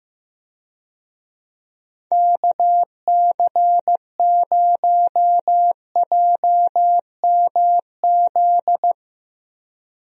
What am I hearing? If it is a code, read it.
Morse code, heard as a keyed tone, about 15 wpm, 700 Hz. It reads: KC0JMZ
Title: USA Call Signs - 6 Characters - 15wpm